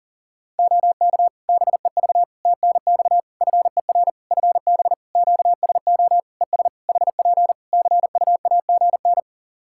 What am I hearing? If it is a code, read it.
Morse code, heard as a keyed tone, about 40 wpm, 700 Hz. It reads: OK BEV TNX FER FB QSO ES HP CUAGN
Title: Practice QSO - Exchange 4 - Ending - 40wpm